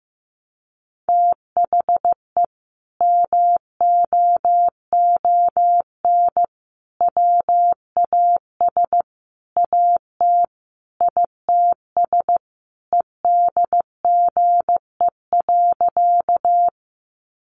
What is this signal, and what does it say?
Morse code, heard as a keyed tone, about 15 wpm, 700 Hz. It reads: THE MOON WAS AT ITS EDGE.
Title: Sentences from Top 1300 Words 15wpm